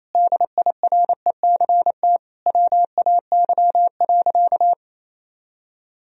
Morse code, heard as a keyed tone, about 28 wpm, 700 Hz. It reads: DIRECT WAY.
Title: Sentences from Top 1000 Words - Review Part 2 - 28wpm